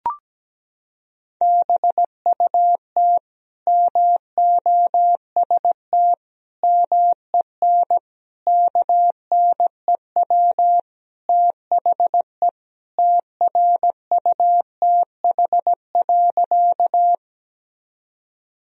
Morse code, heard as a keyed tone, about 17 wpm, 700 Hz. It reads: BUT MOST MEN KNEW THE TRUTH.
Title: Sentences from Top 1000 Words - Review Part 1 - 17wpm